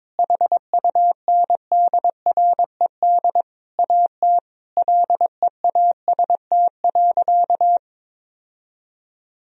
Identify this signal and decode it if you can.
Morse code, heard as a keyed tone, about 22 wpm, 700 Hz. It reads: HUNDRED AT LEAST.
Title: Sentences from Top 1000 Words - Review Part 2 - 22wpm